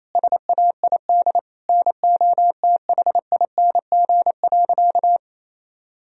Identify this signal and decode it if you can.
Morse code, heard as a keyed tone, about 28 wpm, 700 Hz. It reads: SAID NOTHING.
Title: Sentences from Top 1000 Words - Review Part 2 - 28wpm